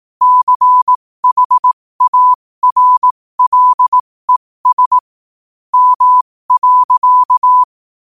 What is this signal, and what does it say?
Morse code, heard as a keyed tone, about 18 wpm, spaced as for 15 wpm, 1000 Hz. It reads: CHARLES M.
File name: qotdfiles / 15wpm